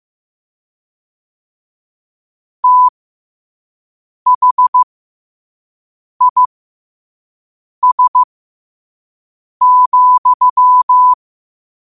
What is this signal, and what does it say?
Morse code, heard as a keyed tone, about 15 wpm, spaced as for 5 wpm, 1000 Hz. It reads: THIS,
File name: qotdfiles / 5wpm